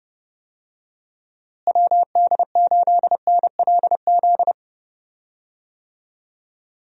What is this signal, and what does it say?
Morse code, heard as a keyed tone, about 30 wpm, 700 Hz. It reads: WD8NLZ
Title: USA Call Signs - 6 Characters - Repeated 2x - 30wpm